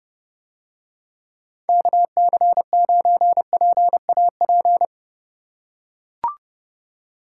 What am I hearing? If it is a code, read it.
Morse code, heard as a keyed tone, about 29 wpm, 700 Hz. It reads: KC9PAP